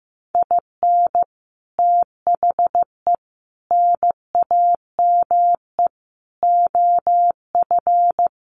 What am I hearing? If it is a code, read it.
Morse code, heard as a keyed tone, about 15 wpm, 700 Hz. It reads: IN THE NAME OF
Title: Sets of 4 Words 15wpm